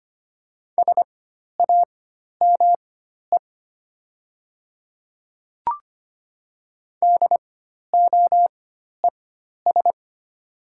Morse code, heard as a keyed tone, about 25 wpm, spaced as for 12 wpm, 700 Hz. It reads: SAME DOES
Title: 4x Character Spacing - Top 200 Words - 25wpm